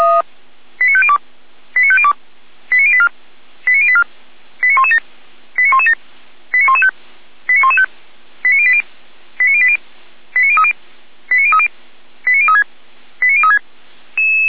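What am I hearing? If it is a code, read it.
Five-tone selective call (ZVEI), heard as quick runs of five tones, tones of about 70 ms. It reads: 89502 89502 89084 89084 89117 89117 89115 89115 89080 89080 89030 89030 89036 89036
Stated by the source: ZVEI five-tone selective call received on SDR, from feuerwehr-orlamuende.org.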